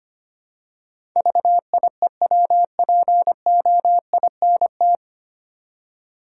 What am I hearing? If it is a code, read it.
Morse code, heard as a keyed tone, about 25 wpm, 700 Hz. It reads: VIEWPOINT
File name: Nine Letter Words - 25wpm